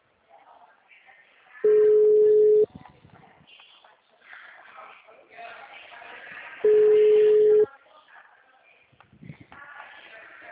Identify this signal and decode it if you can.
Ringback tone (Europe, 425 Hz), with speech.
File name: out- -2048- - - .23018